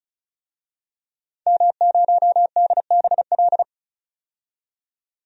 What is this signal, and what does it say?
Morse code, heard as a keyed tone, about 34 wpm, 700 Hz. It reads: M0DBL
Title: DXCC-100 Call Signs - 5 Characters - 35wpm